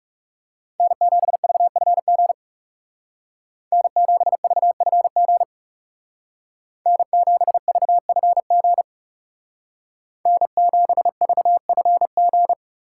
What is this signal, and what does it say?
Morse code, heard as a keyed tone, about 38 wpm, 700 Hz. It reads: N7VFG N7VFG N7VFG N7VFG